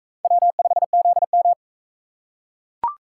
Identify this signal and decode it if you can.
Morse code, heard as a keyed tone, about 39 wpm, 700 Hz. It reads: W5ZM